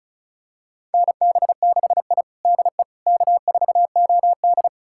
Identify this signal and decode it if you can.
Morse code, heard as a keyed tone, about 36 wpm, 700 Hz. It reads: NB6I DE K4OD
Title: USA Callsign DE Callsign - 4 Characters - 35wpm